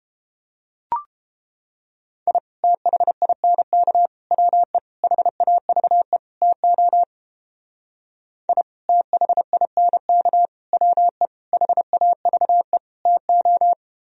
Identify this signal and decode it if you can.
Morse code, heard as a keyed tone, about 31 wpm, 700 Hz. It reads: I THINK WE HAVE TO I THINK WE HAVE TO